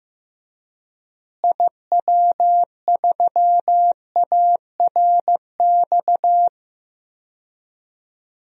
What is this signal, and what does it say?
Morse code, heard as a keyed tone, about 15 wpm, 700 Hz. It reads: IW3ARX
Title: DXCC-100 Call Signs - 6 Characters - Repeated 3x - 15wpm